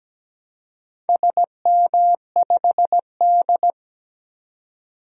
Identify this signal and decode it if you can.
Morse code, heard as a keyed tone, about 17 wpm, 700 Hz. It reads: SM5D
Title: DXCC-100 Call Signs - 4 Characters - 17wpm